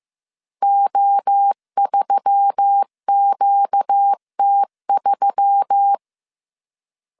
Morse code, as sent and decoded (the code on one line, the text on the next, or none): --- ...-- --.- - ...--
O3QT3